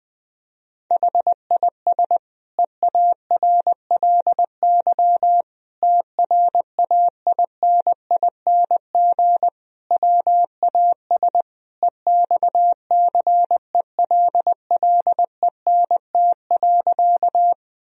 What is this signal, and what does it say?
Morse code, heard as a keyed tone, about 20 wpm, 700 Hz. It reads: HIS EARLY TRAINING WAS EXCELLENT.
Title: Sentences from Top 1000 Words - Review Part 4 - 20wpm